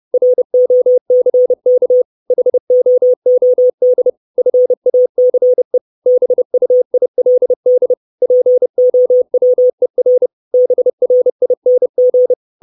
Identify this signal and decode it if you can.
Morse code, heard as a keyed tone, about 30 wpm, 500 Hz. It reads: ROCK HOOD FACE BUILD POWER BRING
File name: 632 Words 30WPM